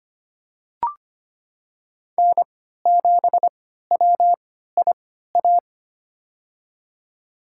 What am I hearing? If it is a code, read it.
Morse code, heard as a keyed tone, about 25 wpm, spaced as for 14 wpm, 700 Hz. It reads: N7WIA
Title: USA Call Signs - 3x Character Spacing - 25wpm